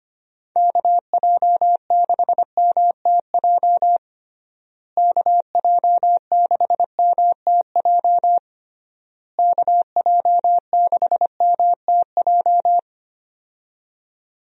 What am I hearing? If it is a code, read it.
Morse code, heard as a keyed tone, about 25 wpm, 700 Hz. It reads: KJ6MTJ KJ6MTJ KJ6MTJ
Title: USA Call Signs - 6 Characters - Repeated 3x - 25wpm